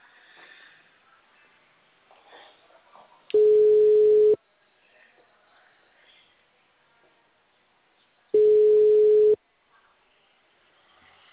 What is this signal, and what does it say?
Ringback tone (Europe, 425 Hz).
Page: Rio Claro 1